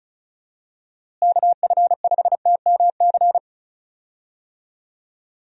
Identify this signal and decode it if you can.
Morse code, heard as a keyed tone, about 34 wpm, 700 Hz. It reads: KF5TMC
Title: USA Call Signs - 6 Characters - Repeated 3x - 35wpm